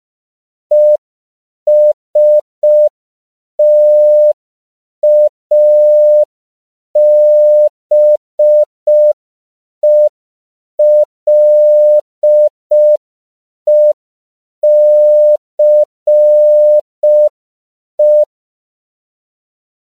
Morse code, heard as a keyed tone, about 5 wpm, 600 Hz. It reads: ESTABELECE
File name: CW / 5PPM